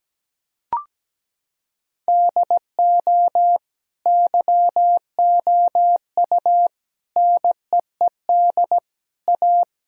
Morse code, heard as a keyed tone, about 17 wpm, 700 Hz. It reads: DO YOU NEED A
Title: Sentences from Top 1000 Words - Review Part 2 - 17wpm